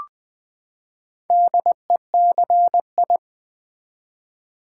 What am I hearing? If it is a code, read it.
Morse code, heard as a keyed tone, about 20 wpm, 700 Hz. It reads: DECI
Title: Build A Word - 4 Letters At A Time - 8-Letter Words - 20wpm